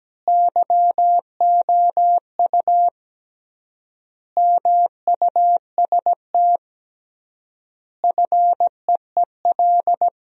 Morse code, heard as a keyed tone, about 17 wpm, 700 Hz. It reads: YOU MUST FEEL
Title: 3x Word-Spacing - Sentences from Top 400 Words - 17wpm